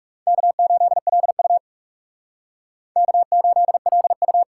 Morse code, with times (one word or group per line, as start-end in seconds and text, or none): 0.3-1.6 K8LU
3.0-4.4 K8LU